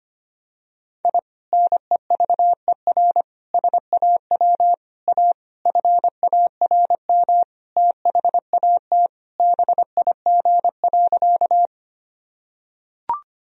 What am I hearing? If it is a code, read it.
Morse code, heard as a keyed tone, about 25 wpm, 700 Hz. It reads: I NEVER SAW A FARM THAT BIG.
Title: Sentences from Top 1000 Words - Review Part 2 - 25wpm